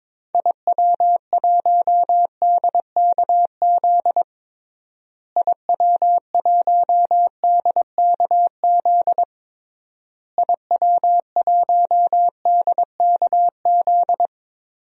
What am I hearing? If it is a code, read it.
Morse code, heard as a keyed tone, about 22 wpm, 700 Hz. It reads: IW1DKZ IW1DKZ IW1DKZ
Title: DXCC-100 Call Signs - 6 Characters - Repeated 3x - 22wpm